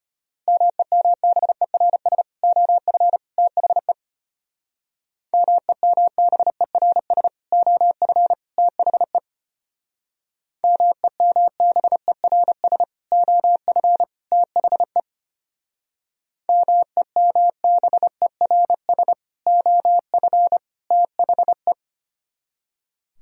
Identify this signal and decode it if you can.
Morse code, heard as a keyed tone, about 33 wpm, 700 Hz. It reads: MEMBERS OF THE MEMBERS OF THE MEMBERS OF THE MEMBERS OF THE